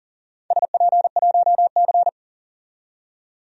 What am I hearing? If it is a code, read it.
Morse code, heard as a keyed tone, about 40 wpm, 700 Hz. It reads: SP1C
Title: DXCC-100 Call Signs - 4 Characters - 40wpm